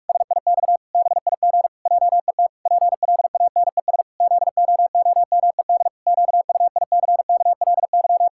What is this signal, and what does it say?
Morse code, heard as a keyed tone, about 41 wpm, 700 Hz. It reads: SIX BIG JET PLANES ZOOMED QUICKLY
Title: Pangram Challenge v2 - 45wpm